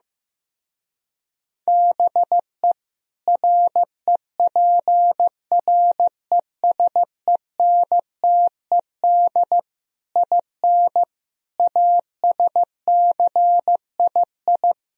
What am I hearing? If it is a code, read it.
Morse code, heard as a keyed tone, about 15 wpm, 700 Hz. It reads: BE REPRESENTED IN ASCII
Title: Speed-Racing - Sets of 5 Words 15wpm